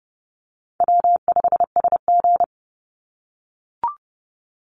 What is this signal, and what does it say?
Morse code, heard as a keyed tone, about 30 wpm, 700 Hz. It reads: W5SG
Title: Easy Call Signs - 30wpm